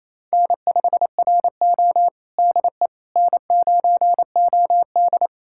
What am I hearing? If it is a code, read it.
Morse code, heard as a keyed tone, about 27 wpm, 690 Hz. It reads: N5RO DE N9OD
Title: USA Callsign DE Callsign - 4 Characters - 28wpm